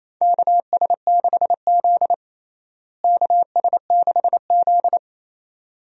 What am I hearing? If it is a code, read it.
Morse code, heard as a keyed tone, about 28 wpm, 700 Hz. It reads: KS6Z KS6Z
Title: USA Call Signs - 4 Characters - Repeated 2x - 28wpm